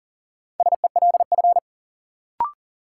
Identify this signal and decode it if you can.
Morse code, heard as a keyed tone, about 40 wpm, 700 Hz. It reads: SELF
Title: Sentences from Top 1000 Words - Review Part 1 - New Word Drill - 40wpm